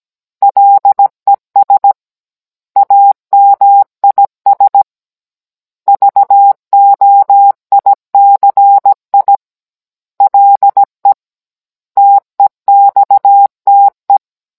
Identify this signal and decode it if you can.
Morse code, heard as a keyed tone, about 17 wpm, 800 Hz. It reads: LES AMIS VOICI LE TEXTE